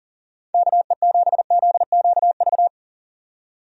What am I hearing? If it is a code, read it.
Morse code, heard as a keyed tone, about 40 wpm, 700 Hz. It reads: KE7ZQV
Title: USA Call Signs - 6 Characters - 40wpm